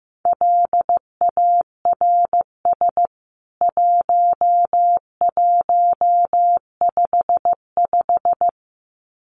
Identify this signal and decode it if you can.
Morse code, heard as a keyed tone, about 15 wpm, 700 Hz. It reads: LARS 1155
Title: CWT Exchange 15wpm